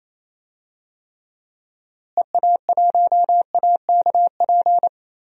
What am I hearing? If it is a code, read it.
Morse code, heard as a keyed tone, about 28 wpm, 700 Hz. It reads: EA1AKP
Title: DXCC-100 Call Signs - 6 Characters - Repeated 3x - 28wpm